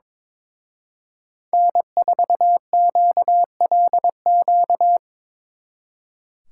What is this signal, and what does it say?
Morse code, heard as a keyed tone, about 25 wpm, 700 Hz. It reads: N4QLQ
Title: Speed Racing - USA 1x3 Call Signs 22wpm